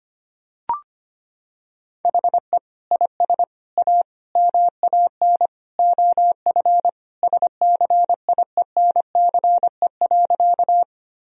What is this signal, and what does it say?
Morse code, heard as a keyed tone, about 25 wpm, 700 Hz. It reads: HE IS A MAN OF SCIENCE.
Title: Sentences from Top 1000 Words - Review Part 2 - 25wpm